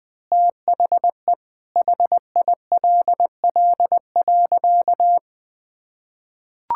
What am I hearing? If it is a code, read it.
Morse code, heard as a keyed tone, about 20 wpm, 700 Hz. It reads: THE HILL.
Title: Sentences with 6-Word Phrases - 20wpm